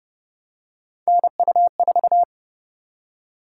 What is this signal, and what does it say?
Morse code, heard as a keyed tone, about 29 wpm, 700 Hz. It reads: NU4